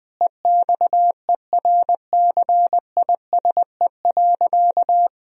Morse code, heard as a keyed tone, about 20 wpm, 700 Hz. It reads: EXERCISE.
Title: Sentences from Top 1000 Words - Review Part 4 - 20wpm